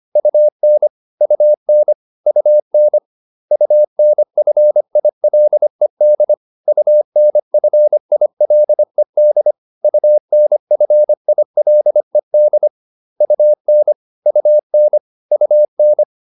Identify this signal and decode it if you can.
Morse code, heard as a keyed tone, about 25 wpm, 600 Hz. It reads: UN UN UN UNFILED UNFILED UNFILED UN UN UN